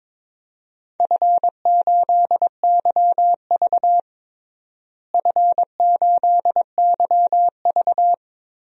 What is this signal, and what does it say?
Morse code, heard as a keyed tone, about 22 wpm, 700 Hz. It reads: F8YV F8YV
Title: DXCC-100 Call Signs - 4 Characters - Repeated 3x - 22wpm